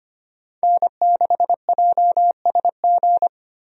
Morse code, heard as a keyed tone, about 25 wpm, 700 Hz. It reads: N6JSG
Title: USA Call Signs - 5 Characters - 25wpm